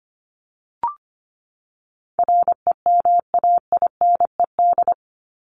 Morse code, heard as a keyed tone, about 25 wpm, 700 Hz. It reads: REMAINED
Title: ED Suffix - Top 100 Words with Base Words 25wpm